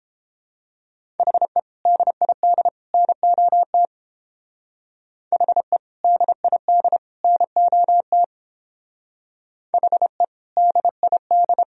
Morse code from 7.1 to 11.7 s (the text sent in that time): NOT HE DID